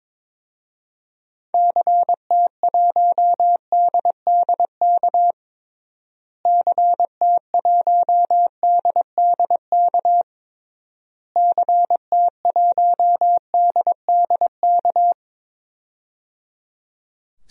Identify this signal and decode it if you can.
Morse code, heard as a keyed tone, about 22 wpm, 700 Hz. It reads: CT1DDK CT1DDK CT1DDK
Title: DXCC-100 Call Signs - 6 Characters - Repeated 3x - 22wpm